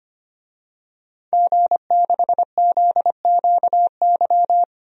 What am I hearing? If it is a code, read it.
Morse code, heard as a keyed tone, about 25 wpm, 700 Hz. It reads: G6ZQY
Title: DXCC-100 Call Signs - 5 Characters - 25wpm